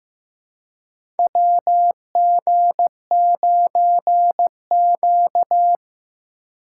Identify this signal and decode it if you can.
Morse code, heard as a keyed tone, about 15 wpm, 700 Hz. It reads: WG9Q
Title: USA Call Signs - 3x Character Spacing - 15wpm